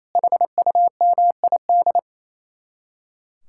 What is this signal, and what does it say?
Morse code, heard as a keyed tone, about 34 wpm, 700 Hz. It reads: HUMID
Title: Speed-Racing Common QSO Phrases - 28wpm